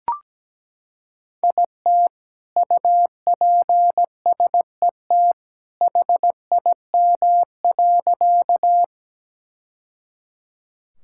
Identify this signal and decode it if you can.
Morse code, heard as a keyed tone, about 17 wpm, 700 Hz. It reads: IT UPSET HIM.
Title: Sentences from Top 1000 Words - Review Part 5 - 17wpm